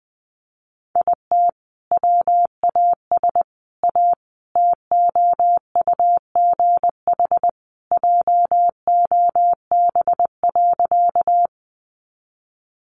Morse code, heard as a keyed tone, about 20 wpm, 700 Hz. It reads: IT WAS A TOUGH JOB.
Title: Sentences from Top 1200 Words 20wpm